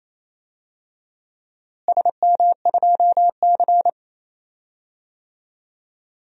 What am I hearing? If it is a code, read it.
Morse code, heard as a keyed tone, about 28 wpm, 700 Hz. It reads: SM2C
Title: DXCC-100 Call Signs - 4 Characters - Repeated 2x - 28wpm